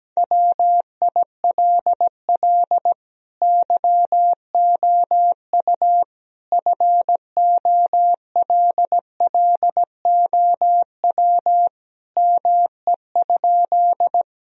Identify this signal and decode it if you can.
Morse code, heard as a keyed tone, about 17 wpm, 700 Hz. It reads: WILL YOU FOLLOW ME?
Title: Sentences from Top 1000 Words - Review Part 1 - 17wpm